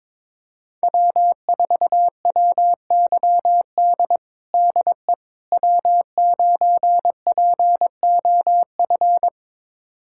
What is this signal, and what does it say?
Morse code, heard as a keyed tone, about 22 wpm, 700 Hz. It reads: W4WYD DE W9POF
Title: USA Callsign DE Callsign - 5 Characters - 22wpm